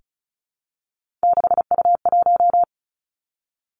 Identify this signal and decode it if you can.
Morse code, heard as a keyed tone, about 36 wpm, 700 Hz. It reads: 6U1
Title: Three Challenging Letter-Number Combinations 35wpm